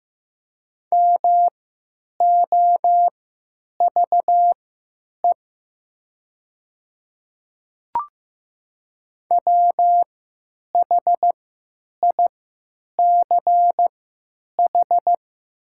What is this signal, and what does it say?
Morse code, heard as a keyed tone, about 15 wpm, spaced as for 9 wpm, 700 Hz. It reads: MOVE WHICH
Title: 3x Character Spacing - Top 300 Words - 15wpm